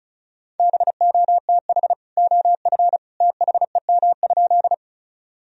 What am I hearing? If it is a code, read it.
Morse code, heard as a keyed tone, about 36 wpm, 700 Hz. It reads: BOTH OF THEM?
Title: Sentences from Top 1000 Words - Review Part 1 - 35wpm